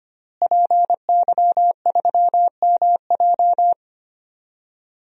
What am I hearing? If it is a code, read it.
Morse code, heard as a keyed tone, about 25 wpm, 700 Hz. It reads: PY3MJ
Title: DXCC-100 Call Signs - 5 Characters - Repeated 2x - 25wpm